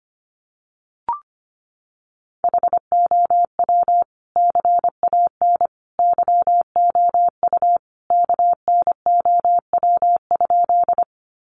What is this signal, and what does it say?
Morse code, heard as a keyed tone, about 25 wpm, 700 Hz. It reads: HOW CAN YOU KNOW?
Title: Sentences from Top 100 Words 25wpm